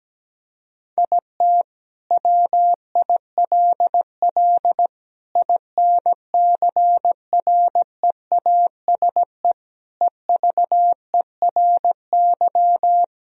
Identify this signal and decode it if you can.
Morse code, heard as a keyed tone, about 17 wpm, 700 Hz. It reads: IT WILL INCREASE EVERY
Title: Sentences from Top 1000 Words - Review Part 4 - 17wpm